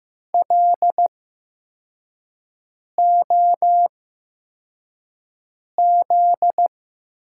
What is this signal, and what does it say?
Morse code, heard as a keyed tone, about 15 wpm, spaced as for 4 wpm, 700 Hz. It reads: LOZ
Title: USA Call Signs - 6 Characters - 8x Character Spacing - 15wpm